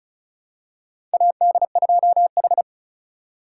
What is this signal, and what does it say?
Morse code, heard as a keyed tone, about 36 wpm, 700 Hz. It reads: AD2H
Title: USA Call Signs - 4 Characters - 3x Character Spacing - 35wpm